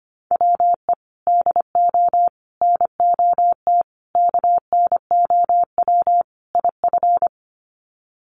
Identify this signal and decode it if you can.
Morse code, heard as a keyed tone, about 25 wpm, 700 Hz. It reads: WE DO NOT KNOW IF
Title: Sets of 5 Words Encore2 25wpm